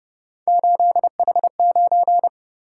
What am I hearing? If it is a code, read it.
Morse code, heard as a keyed tone, about 29 wpm, 700 Hz. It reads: 8H9